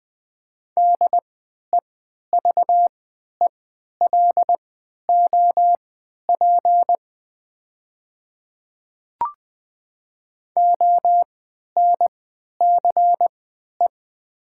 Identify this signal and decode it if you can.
Morse code, heard as a keyed tone, about 20 wpm, spaced as for 11 wpm, 690 Hz. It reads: DEVELOP ONCE
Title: 3x Character Spacing - Top 500 Words - 20wpm